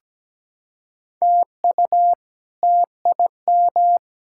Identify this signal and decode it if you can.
Morse code, heard as a keyed tone, about 17 wpm, 700 Hz. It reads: TU TIM
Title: K1USN SST QSO Practice - 17wpm